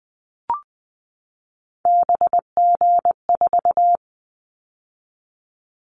Morse code, heard as a keyed tone, about 20 wpm, 700 Hz. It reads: BG4
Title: Callsign Prefixes for Top 100 DXCC - 20wpm